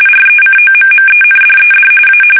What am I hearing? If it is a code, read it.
RTTY, about 45.45 baud. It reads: RECOMMENDED A
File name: fsk850